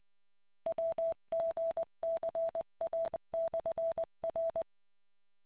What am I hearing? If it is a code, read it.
Morse code, heard as a keyed tone, about 20 wpm, 660 Hz. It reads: WGCA/R